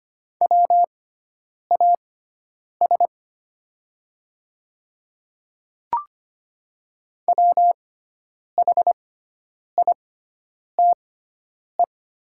Morse code, heard as a keyed tone, about 25 wpm, spaced as for 9 wpm, 700 Hz. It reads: WAS WHITE
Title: 6x Character Spacing - Top 300 Words - 25wpm